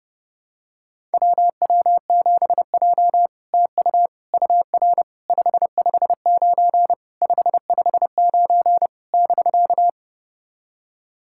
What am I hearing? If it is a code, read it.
Morse code, heard as a keyed tone, about 30 wpm, 700 Hz. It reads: WW7J TU UR 559 559 <BK>
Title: SOTA QSO Practice 30wpm